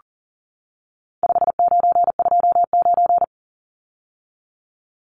Morse code, heard as a keyed tone, about 40 wpm, 700 Hz. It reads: 5929
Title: Numbers - 4-Digit 40wpm